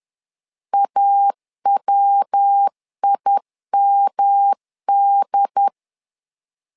Morse code, heard as a keyed tone, about 10 wpm, 800 Hz. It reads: AWIMD
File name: LEZIONI / 4 Q 10wpm 2210